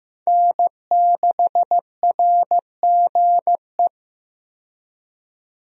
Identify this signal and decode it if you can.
Morse code, heard as a keyed tone, about 15 wpm, 700 Hz. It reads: N6RGE
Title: USA Call Signs - 5 Characters - 15wpm